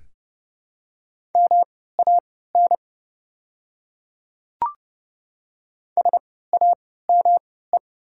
Morse code, heard as a keyed tone, about 31 wpm, spaced as for 17 wpm, 700 Hz. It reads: MAN SAME
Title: 3x Character Spacing - Top 200 Words - 30wpm